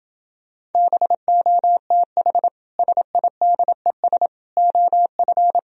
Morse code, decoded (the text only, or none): BOTH SIDES OF